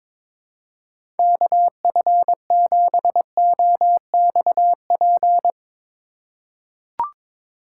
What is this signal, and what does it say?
Morse code, heard as a keyed tone, about 22 wpm, 700 Hz. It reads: KF7OXP
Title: USA Call Signs - 3x Character Spacing - 22wpm